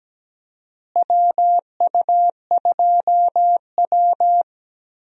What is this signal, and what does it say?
Morse code, heard as a keyed tone, about 17 wpm, 700 Hz. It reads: WU2W
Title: USA Call Signs - 4 Characters - Repeated 3x - 17wpm